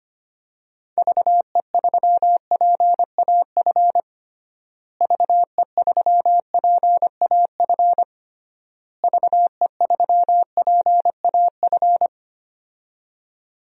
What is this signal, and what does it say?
Morse code, heard as a keyed tone, about 25 wpm, 700 Hz. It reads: VE3PAF VE3PAF VE3PAF
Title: DXCC-100 Call Signs - 6 Characters - Repeated 3x - 25wpm